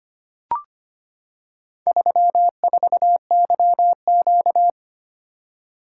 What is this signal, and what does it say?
Morse code, heard as a keyed tone, about 25 wpm, 700 Hz. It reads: 34YQ
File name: practice / 25wpm / Four Letter-Number Combinations - 25wpm